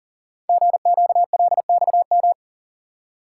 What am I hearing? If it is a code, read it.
Morse code, heard as a keyed tone, about 41 wpm, 700 Hz. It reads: GQLXM
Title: Code Groups - 5 Letters - 40wpm